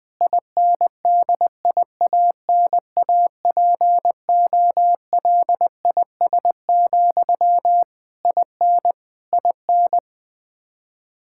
Morse code, heard as a keyed tone, about 20 wpm, 700 Hz. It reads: INDIANAPOLIS, IN IN
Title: Practice QSO - Exchange 1 - Beginning - 20wpm